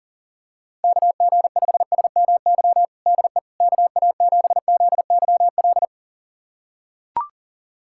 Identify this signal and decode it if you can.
Morse code, heard as a keyed tone, about 37 wpm, 700 Hz. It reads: KG5SMY DE KA7ZYL
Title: USA Callsign DE Callsign - 6 Characters - 40wpm